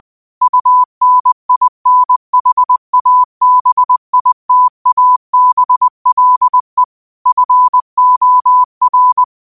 Morse code, heard as a keyed tone, about 20 wpm, 1000 Hz. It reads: UNINHABITABLE FOR